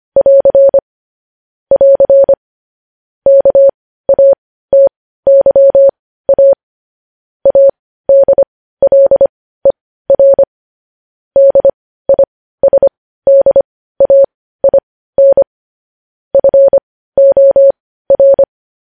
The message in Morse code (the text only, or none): <AR> <AR> KATYA ADLER DISDAIN FOR